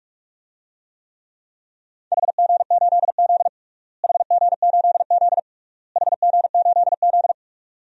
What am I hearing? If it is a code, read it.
Morse code, heard as a keyed tone, about 42 wpm, 700 Hz. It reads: HG8Z HG8Z HG8Z
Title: DXCC-100 Call Signs - 4 Characters - Repeated 3x - 45wpm